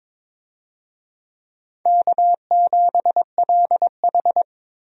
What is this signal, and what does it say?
Morse code, heard as a keyed tone, about 21 wpm, 700 Hz. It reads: K7LH
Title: USA Call Signs - 4 Characters - Repeated 2x - 22wpm